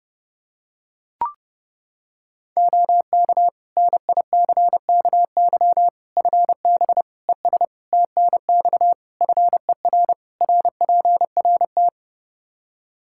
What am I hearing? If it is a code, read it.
Morse code, heard as a keyed tone, about 30 wpm, 700 Hz. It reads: OK NICKY FB ES TNX FER RPRT
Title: Practice QSO - Exchange 2 - Info - 30wpm